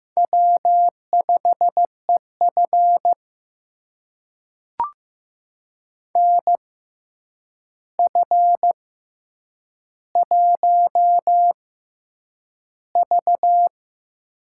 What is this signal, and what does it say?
Morse code, heard as a keyed tone, about 15 wpm, spaced as for 8 wpm, 700 Hz. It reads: W5EF NF1V
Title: USA Call Signs - 4 Characters - 6x Character Spacing - 15wpm